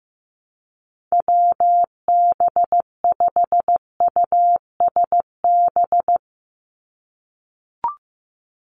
Morse code, heard as a keyed tone, about 15 wpm, 700 Hz. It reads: WB5USB
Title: NA SOTA Chaser Callsigns 15wpm